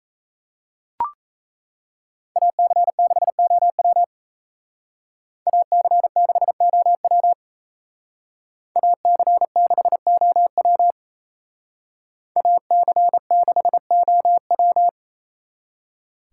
Morse code, heard as a keyed tone, about 36 wpm, 700 Hz. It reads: AC6OW AC6OW AC6OW AC6OW